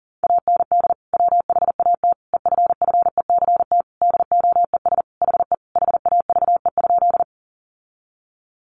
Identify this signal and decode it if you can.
Morse code, heard as a keyed tone, about 40 wpm, 700 Hz. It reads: AND WHAT EFFECT DOES HE HAVE?
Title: Sentences from Top 1000 Words 40wpm